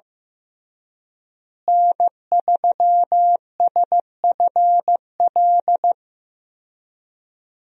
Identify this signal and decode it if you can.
Morse code, heard as a keyed tone, about 14 wpm, 700 Hz. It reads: N3SFL
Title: USA Call Signs - 5 Characters - Repeated 3x - 15wpm